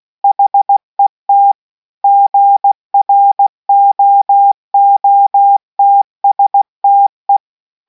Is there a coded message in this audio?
Yes: Morse code from 0.2 to 7.4 s, heard as a keyed tone, about 16 wpm, 800 Hz. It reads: HET GROOTSTE